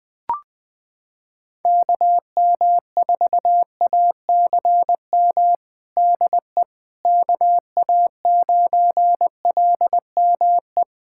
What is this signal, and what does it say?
Morse code, heard as a keyed tone, about 21 wpm, 700 Hz. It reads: KM4ACM DE KA9LME
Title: USA Callsign DE Callsign 20wpm